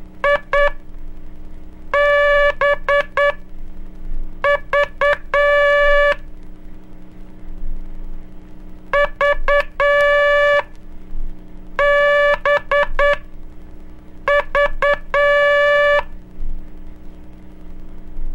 Morse code, heard as a keyed tone, about 9 wpm, spaced as for 5 wpm, 580 Hz. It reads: IBV VBV
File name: cursos / Curso CW mp3 / CW Aula - 06